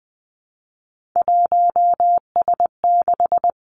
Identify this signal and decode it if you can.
Morse code, heard as a keyed tone, about 20 wpm, 700 Hz. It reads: 1S6
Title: Three Challenging Letter-Number Combinations 20wpm